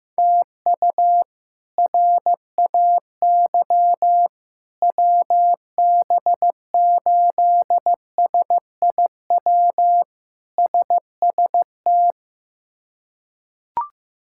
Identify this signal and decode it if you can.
Morse code, heard as a keyed tone, about 15 wpm, 700 Hz. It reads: TU RAY WB8SIW SST
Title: K1USN SST QSO Practice - 15wpm